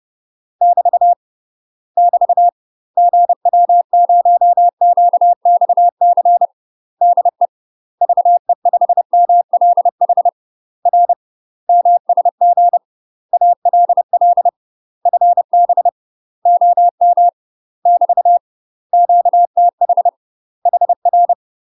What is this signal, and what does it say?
Morse code, heard as a keyed tone, about 30 wpm, 700 Hz. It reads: <BT> <BT> GW0QXC DE VE5MLH R MSG ALL FB OM <BT> QTH HR
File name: UK MASTER QSOs TRAINING / 30WPM / UK MASTER QSOs 0040